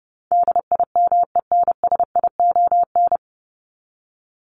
Morse code, heard as a keyed tone, about 30 wpm, 700 Hz. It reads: DIMENSION
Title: Top 1501-1750 Words 30wpm